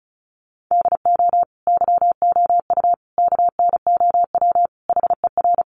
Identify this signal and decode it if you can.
Morse code, heard as a keyed tone, about 34 wpm, 700 Hz. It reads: DO YOU KNOW HER
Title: Sentences from Top 1400 Words 35wpm